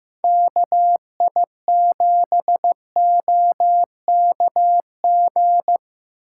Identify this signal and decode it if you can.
Morse code, heard as a keyed tone, about 15 wpm, 700 Hz. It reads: KI7OKG
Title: USA Call Signs - 2x Character Spacing - 15wpm